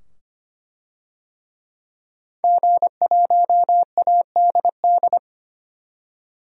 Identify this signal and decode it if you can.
Morse code, heard as a keyed tone, about 25 wpm, 700 Hz. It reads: G1ADD
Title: DXCC-100 Call Signs - 5 Characters - 25wpm